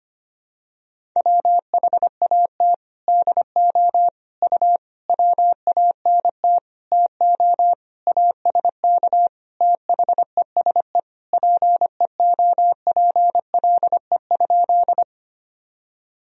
Morse code, heard as a keyed tone, about 25 wpm, 700 Hz. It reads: WHAT DO U WANT TO ASK THESE PEOPLE?
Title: Sentences from Top 200 Words and Abbreviations 25wpm